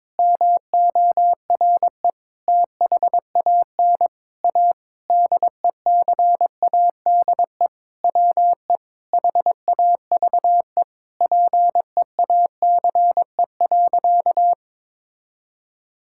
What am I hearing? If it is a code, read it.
Morse code, heard as a keyed tone, about 22 wpm, 700 Hz. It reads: MORE THAN A DECADE WE HAVE PEACE.
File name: Sentences with 9-Word Phrases - 22wpm